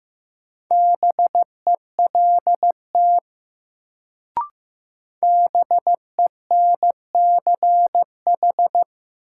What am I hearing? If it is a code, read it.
Morse code, heard as a keyed tone, about 15 wpm, 700 Hz. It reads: BELT BENCH
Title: Sentences from Top 1400 Words - New Word Drill - 15wpm